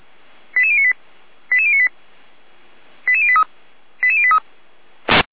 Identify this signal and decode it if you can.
Five-tone selective call (ZVEI), heard as quick runs of five tones, tones of about 70 ms. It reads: 88098 88098 88083 88083